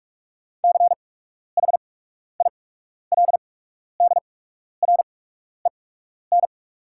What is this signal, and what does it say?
Morse code, heard as a keyed tone, about 48 wpm, spaced as for 12 wpm, 700 Hz. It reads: CHILDREN
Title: 8x Character Spacing - Top 300 Words - 45wpm